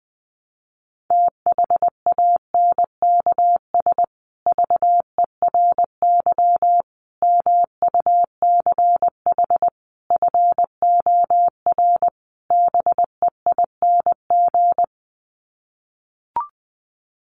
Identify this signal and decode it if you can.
Morse code, heard as a keyed tone, about 20 wpm, 700 Hz. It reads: THANKS VERY MUCH FOR BEING
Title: Sets of 5 Words Encore 20wpm